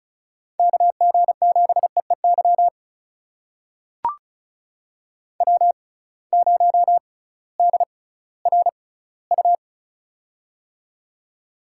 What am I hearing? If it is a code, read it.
Morse code, heard as a keyed tone, about 36 wpm, spaced as for 19 wpm, 700 Hz. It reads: KG7EEY W0DRU
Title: USA Call Signs - 6x Character Spacing - 35wpm